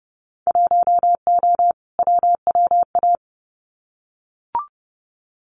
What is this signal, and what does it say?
Morse code, heard as a keyed tone, about 30 wpm, 700 Hz. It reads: 1O WWA
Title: Winter Field Day Exchanges 30wpm